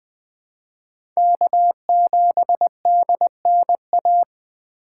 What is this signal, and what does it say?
Morse code, heard as a keyed tone, about 20 wpm, 700 Hz. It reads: K7DNA
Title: USA Call Signs - 5 Characters - 2x Character Spacing - 20wpm